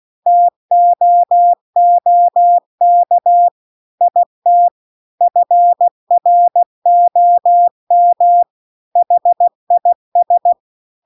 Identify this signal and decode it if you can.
Morse code, heard as a keyed tone, about 16 wpm, 700 Hz. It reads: TOOK IT FROM HIS
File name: CW-The-Big-Twoheaded-River-2-16WPM